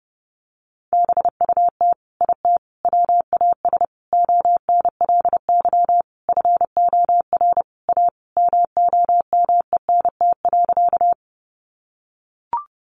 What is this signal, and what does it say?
Morse code, heard as a keyed tone, about 30 wpm, 700 Hz. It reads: BUT IT WAS ONLY FOR A MOMENT.
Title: Sentences from Top 1000 Words 30wpm